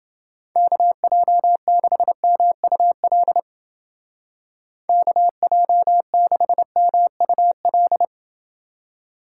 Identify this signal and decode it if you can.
Morse code, heard as a keyed tone, about 30 wpm, 700 Hz. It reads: KJ6MUL KJ6MUL